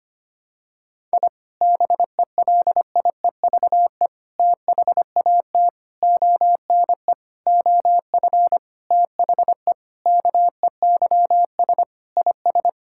Morse code, heard as a keyed tone, about 25 wpm, 700 Hz. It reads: I BELIEVE THAT ONE OF THE KEYS IS
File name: practice / 25wpm / Sentences with 4-Word Phrases - 25wpm